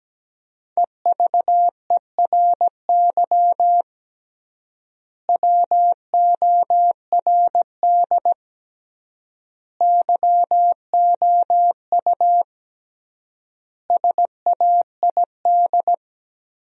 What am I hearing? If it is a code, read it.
Morse code, heard as a keyed tone, about 17 wpm, 700 Hz. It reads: EVERY WORD YOU SAID
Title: 3x Word-Spacing - Sentences from Top 400 Words - 17wpm